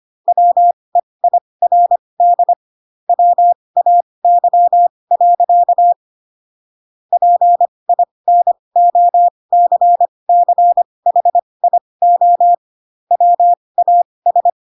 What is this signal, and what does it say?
Morse code, heard as a keyed tone, about 25 wpm, spaced as for 20 wpm, 700 Hz. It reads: WEIRD WAY. PINOCCHIO WAS
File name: Pinocchio / Pinocchio ch6 15 25 ik1hge